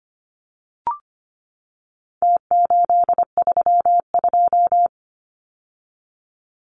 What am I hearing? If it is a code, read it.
Morse code, heard as a keyed tone, about 25 wpm, 700 Hz. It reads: T832
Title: Numbers - Contest Style - 4-Digit 25wpm